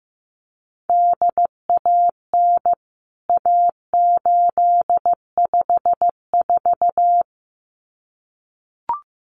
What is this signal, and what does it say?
Morse code, heard as a keyed tone, about 15 wpm, 700 Hz. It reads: DAN A854
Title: CWT Exchange 15wpm